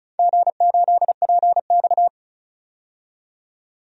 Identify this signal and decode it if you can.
Morse code, heard as a keyed tone, about 36 wpm, 700 Hz. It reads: G8PX
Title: DXCC-100 Call Signs - 4 Characters - 35wpm